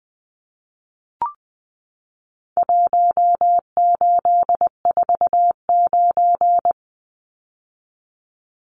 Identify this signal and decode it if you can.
Morse code, heard as a keyed tone, about 20 wpm, 700 Hz. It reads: 1849
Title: Numbers - Contest Style - 4-Digit 20wpm